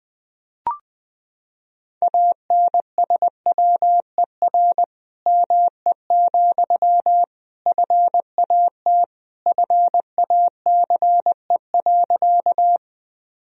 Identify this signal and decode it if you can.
Morse code, heard as a keyed tone, about 20 wpm, 700 Hz. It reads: ANSWER ME, FAT FACE.
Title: Sentences from Top 1000 Words - Review Part 4 - 20wpm